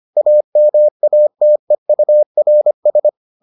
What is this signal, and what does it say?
Morse code, heard as a keyed tone, about 25 wpm, 600 Hz. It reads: AMATEURS